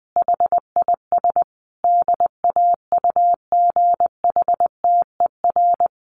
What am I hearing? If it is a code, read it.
Morse code, heard as a keyed tone, about 20 wpm, 700 Hz. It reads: HIS DAUGHTER
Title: Sentences from Top 1500 Words 20wpm